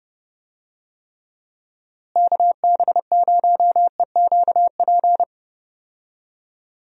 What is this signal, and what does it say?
Morse code, heard as a keyed tone, about 31 wpm, 700 Hz. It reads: KB0EQP
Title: USA Call Signs - 6 Characters - Repeated 2x - 30wpm